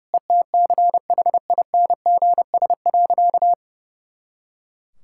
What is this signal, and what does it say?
Morse code, heard as a keyed tone, about 30 wpm, 700 Hz. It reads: ETCHINGS.
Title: Pangram Challenge v2 - 30wpm